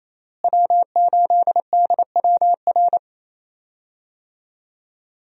Morse code, heard as a keyed tone, about 28 wpm, 700 Hz. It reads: W8DWR
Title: USA Call Signs - 5 Characters - Repeated 3x - 28wpm